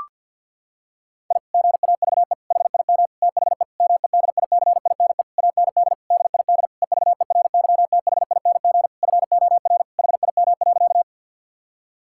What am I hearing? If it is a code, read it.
Morse code, heard as a keyed tone, about 50 wpm, 700 Hz. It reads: I GAVE HIM THE MEDICINE AND DID EVERYTHING FOR HIM.
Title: Sentences from Top 1000 Words - Review Part 6 - 50wpm